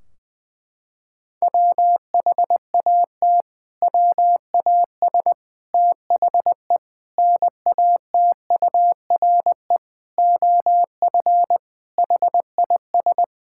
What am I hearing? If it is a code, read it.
Morse code, heard as a keyed tone, about 21 wpm, 700 Hz. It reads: WHAT WAS THE NATURE OF HIS
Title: Sentences from Top 1600 Words 20wpm